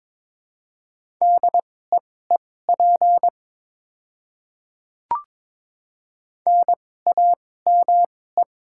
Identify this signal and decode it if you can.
Morse code, heard as a keyed tone, about 22 wpm, spaced as for 16 wpm, 700 Hz. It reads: DEEP NAME
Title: 2x Character Spacing - Top 500 Words - 22wpm